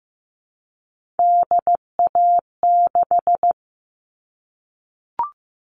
Morse code, heard as a keyed tone, about 15 wpm, 700 Hz. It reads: DA6
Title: All Cut Numbers - 3-Digit 15wpm